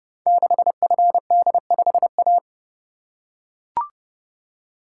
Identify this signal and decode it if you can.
Morse code, heard as a keyed tone, about 30 wpm, 700 Hz. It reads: 6FD5A
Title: Code Groups - 5 Letter:Numbers - 30wpm